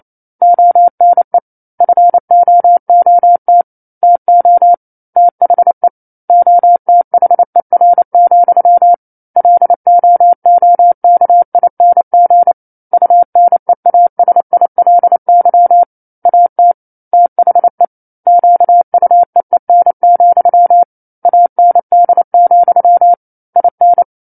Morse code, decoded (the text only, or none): ONE FOOT TO THE OTHER, LOOKING UNEASILY AT THE QUEEN, AND, IN